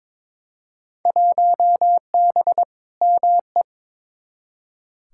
Morse code, heard as a keyed tone, about 22 wpm, 700 Hz. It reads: 1B ME
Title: Field Day Exchanges - 22wpm